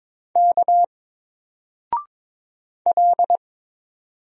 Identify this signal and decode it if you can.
Morse code, heard as a keyed tone, about 22 wpm, 700 Hz. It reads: K L
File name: Single-Letter-Number-22wpm